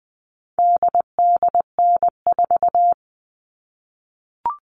Morse code, heard as a keyed tone, about 19 wpm, 700 Hz. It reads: DDN4
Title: All Cut Numbers - 4-Digit 20wpm